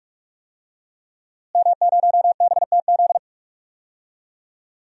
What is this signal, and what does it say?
Morse code, heard as a keyed tone, about 46 wpm, 700 Hz. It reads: M0BTZ
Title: DXCC-100 Call Signs - 5 Characters - Repeated 3x - 45wpm